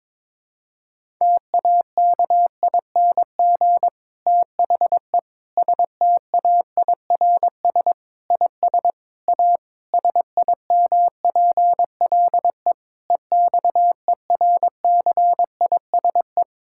Morse code, heard as a keyed tone, about 22 wpm, 700 Hz. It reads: TAKING THE STAIRS IS A SIMPLE EXERCISE
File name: Difficult Sentences with Hints - 22wpm